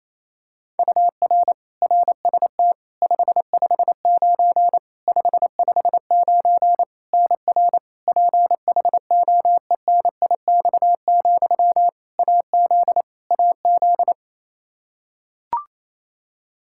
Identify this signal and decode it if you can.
Morse code, heard as a keyed tone, about 28 wpm, 700 Hz. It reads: UR RST 559 559 NR PHOENIX, AZ AZ
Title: Practice QSO - Exchange 1 - Beginning - 28wpm